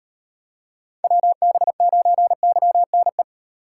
Morse code, heard as a keyed tone, about 36 wpm, 700 Hz. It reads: WB9YNE